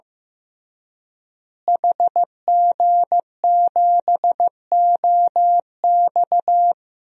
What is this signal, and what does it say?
Morse code, heard as a keyed tone, about 15 wpm, 700 Hz. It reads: HG7OX
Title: DXCC-100 Call Signs - 5 Characters - Repeated 2x - 15wpm